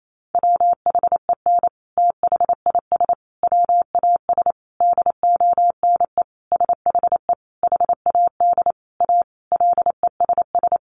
Morse code, heard as a keyed tone, about 28 wpm, 700 Hz. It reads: WHEN THIS WAS DONE SHE HAD A LESS
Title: Sentences from Top 900 Words 28wpm